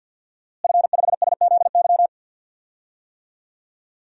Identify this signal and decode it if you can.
Morse code, heard as a keyed tone, about 51 wpm, 700 Hz. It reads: F5SZY
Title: DXCC-100 Call Signs - 5 Characters - Repeated 3x - 50wpm